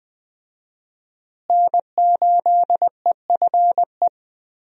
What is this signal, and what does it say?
Morse code, heard as a keyed tone, about 20 wpm, 700 Hz. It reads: N8EFE
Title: USA Call Signs - 5 Characters - Repeated 3x - 20wpm